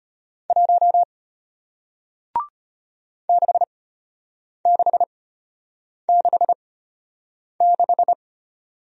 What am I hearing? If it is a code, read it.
Morse code, heard as a keyed tone, about 33 wpm, spaced as for 8 wpm, 700 Hz. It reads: 1 6666